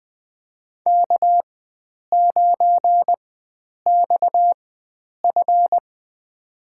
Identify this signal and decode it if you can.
Morse code, heard as a keyed tone, about 21 wpm, spaced as for 9 wpm, 700 Hz. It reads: K9XF
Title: USA Call Signs - 4 Characters - 4x Character Spacing - 20wpm